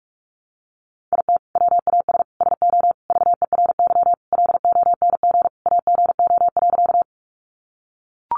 Morse code, heard as a keyed tone, about 44 wpm, 700 Hz. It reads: IT WAS SO VERY LONG AGO.
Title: Sentences from Top 500 Words 45wpm